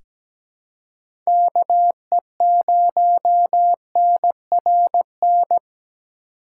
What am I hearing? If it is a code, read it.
Morse code, heard as a keyed tone, about 17 wpm, 700 Hz. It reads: KE0NRN
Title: USA Call Signs - 6 Characters - 8x Character Spacing - 17wpm